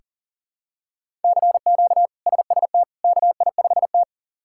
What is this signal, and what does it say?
Morse code, heard as a keyed tone, about 40 wpm, 700 Hz. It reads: CQ SST KI5T
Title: K1USN SST QSO Practice - 40wpm